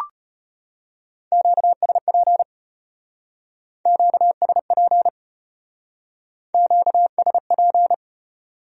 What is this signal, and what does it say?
Morse code, heard as a keyed tone, about 34 wpm, 700 Hz. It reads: QSP QSP QSP